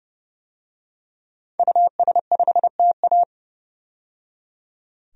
Morse code, heard as a keyed tone, about 31 wpm, 700 Hz. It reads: US5TA
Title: DXCC-100 Call Signs - 5 Characters - 30wpm